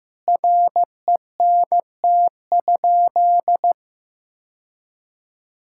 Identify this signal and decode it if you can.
Morse code, heard as a keyed tone, about 15 wpm, 700 Hz. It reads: RENT?
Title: Sentences from Top 1000 Words - Review Part 5 - 15wpm